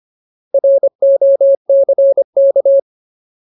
Morse code, heard as a keyed tone, about 25 wpm, 550 Hz. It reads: ROCK
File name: Ultimate Word List / 25WPM Ultimate